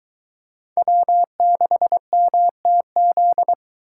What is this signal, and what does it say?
Morse code, heard as a keyed tone, about 21 wpm, 700 Hz. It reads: W6MTZ